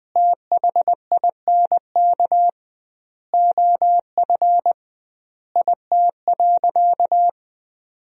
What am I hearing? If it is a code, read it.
Morse code, heard as a keyed tone, about 20 wpm, 700 Hz. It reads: THINK OF IT.
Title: 2x Word-Spacing - Sentences from Top 200 Words - 20wpm